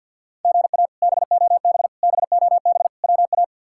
Morse code, heard as a keyed tone, about 53 wpm, 700 Hz. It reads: GA BOB BOB WA
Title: K1USN SST QSO Practice - 50wpm